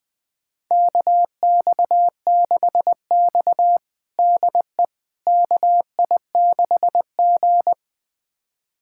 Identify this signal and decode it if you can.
Morse code, heard as a keyed tone, about 20 wpm, 700 Hz. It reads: KX6X DE KI6G
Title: USA Callsign DE Callsign - 4 Characters - 20wpm